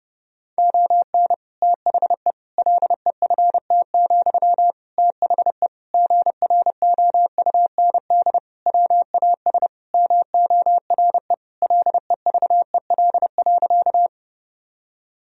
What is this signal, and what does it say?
Morse code, heard as a keyed tone, about 32 wpm, 700 Hz. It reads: ON THE LEFT, THE GROUND WAS MORE LEVEL.
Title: Sentences from Top 1000 Words - Review Part 4 - 30wpm